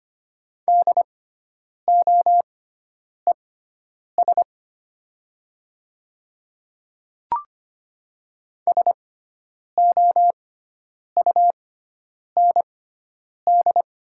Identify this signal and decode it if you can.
Morse code, heard as a keyed tone, about 25 wpm, spaced as for 9 wpm, 700 Hz. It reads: DOES SOUND
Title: 6x Character Spacing - Top 200 Words - 25wpm